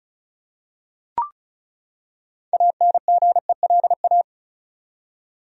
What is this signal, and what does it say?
Morse code, heard as a keyed tone, about 34 wpm, 700 Hz. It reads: ANGELA
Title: USA Common Names - 35wpm